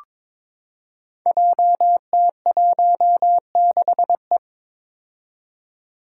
Morse code, heard as a keyed tone, about 22 wpm, 700 Hz. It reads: JT16E
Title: Five Challenging Letter-Number Combinations 22wpm